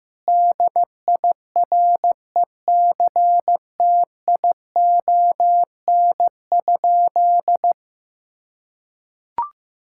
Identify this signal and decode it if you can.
Morse code, heard as a keyed tone, about 15 wpm, 700 Hz. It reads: DIRECTION?
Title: Sentences from Top 1000 Words - Review Part 5 - 15wpm